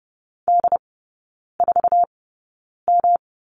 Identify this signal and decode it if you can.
Morse code, heard as a keyed tone, about 30 wpm, spaced as for 9 wpm, 700 Hz. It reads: D4M
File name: Lesson 18e - Letter m - Three Characters 30wpm